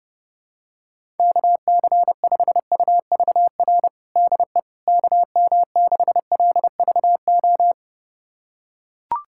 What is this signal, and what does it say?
Morse code, heard as a keyed tone, about 30 wpm, 700 Hz. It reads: KC5UVR DE KM6LVO
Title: USA Callsign DE Callsign 30wpm